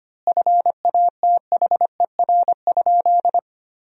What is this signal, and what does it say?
Morse code, heard as a keyed tone, about 25 wpm, 700 Hz. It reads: FATHER?
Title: 3x Word-Spacing - Sentences from Top 400 Words - 25wpm